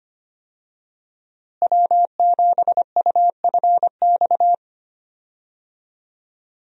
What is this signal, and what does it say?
Morse code, heard as a keyed tone, about 25 wpm, 700 Hz. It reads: W7UFX
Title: USA Call Signs - 5 Characters - Repeated 2x - 25wpm